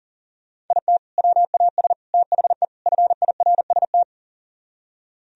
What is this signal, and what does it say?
Morse code, heard as a keyed tone, about 51 wpm, 700 Hz. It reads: IT WAS THE FIRST